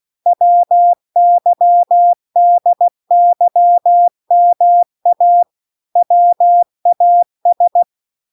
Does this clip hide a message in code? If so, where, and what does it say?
Yes: Morse code from 0.3 to 7.8 s, heard as a keyed tone, about 16 wpm, 700 Hz. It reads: WYDYMA WAS